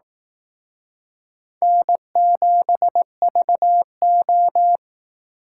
Morse code, heard as a keyed tone, about 18 wpm, 700 Hz. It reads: N7VO
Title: Speed Racing - USA 1x2 Call Signs 15wpm